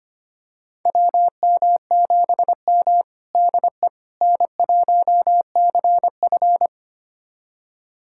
Morse code, heard as a keyed tone, about 25 wpm, 700 Hz. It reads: WM7M DE N1CF
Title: USA Callsign DE Callsign - 4 Characters - 25wpm